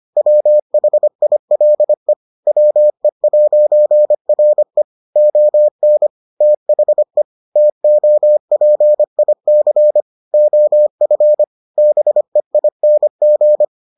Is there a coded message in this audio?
Yes: Morse code from 0.2 to 13.6 s, heard as a keyed tone, about 25 wpm, 600 Hz. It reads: WHILE WE'RE ON THE TOPIC OF BEING